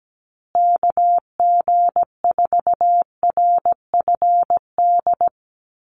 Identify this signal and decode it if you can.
Morse code, heard as a keyed tone, about 17 wpm, 700 Hz. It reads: KG4RFD
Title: USA Call Signs Repeated 3x 17wpm